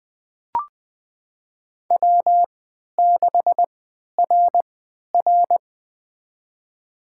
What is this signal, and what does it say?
Morse code, heard as a keyed tone, about 21 wpm, spaced as for 11 wpm, 700 Hz. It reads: W6RR
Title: USA Call Signs - 4 Characters - 3x Character Spacing - 20wpm